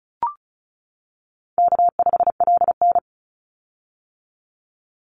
Morse code, heard as a keyed tone, about 34 wpm, 700 Hz. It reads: K5LN
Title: Call Signs - 35wpm